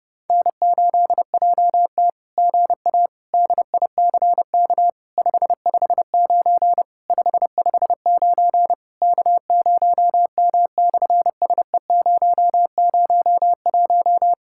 Morse code, heard as a keyed tone, about 30 wpm, 700 Hz. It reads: N8JT GA DICK 559 559 K0M/SE001
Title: SOTA QSO Practice 30wpm